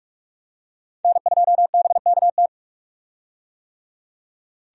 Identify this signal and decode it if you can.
Morse code, heard as a keyed tone, about 46 wpm, 700 Hz. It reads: N2BKT
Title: USA Call Signs - 5 Characters - 45wpm